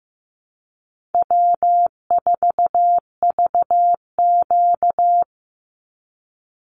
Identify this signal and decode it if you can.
Morse code, heard as a keyed tone, about 15 wpm, 700 Hz. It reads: W4VQ
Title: Easy Call Signs - 15wpm